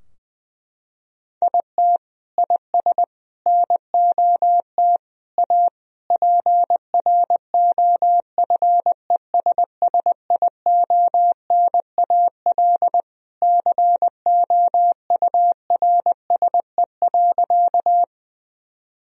Morse code, heard as a keyed tone, about 20 wpm, 700 Hz. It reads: IT IS NOT A PROFESSIONAL COURSE.
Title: Sentences from Top 1000 Words - Review Part 4 - 20wpm